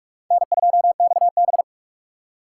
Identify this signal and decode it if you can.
Morse code, heard as a keyed tone, about 45 wpm, 700 Hz. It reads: N2XB
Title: USA Call Signs - 2x Character Spacing - 45wpm